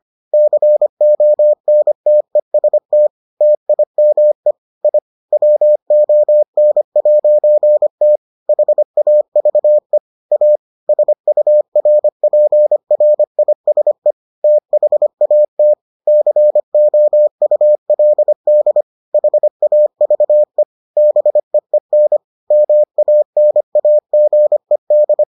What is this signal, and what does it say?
Morse code, heard as a keyed tone, about 25 wpm, 600 Hz. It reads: CONTEST TIME I WON'T HAVE A SURPRISE THAT COULD HAVE BEEN MANAGED